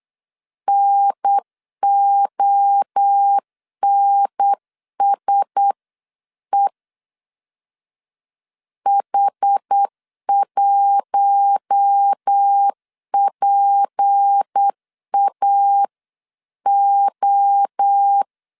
Morse code, heard as a keyed tone, about 8 wpm, 800 Hz. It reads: NONSE H1PAO
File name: CW / DET / 1 6 1500C S B 8wpm 5142